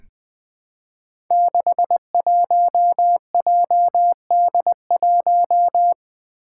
Morse code, heard as a keyed tone, about 20 wpm, 700 Hz. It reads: 61JD1
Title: Five Challenging Letter-Number Combinations 20wpm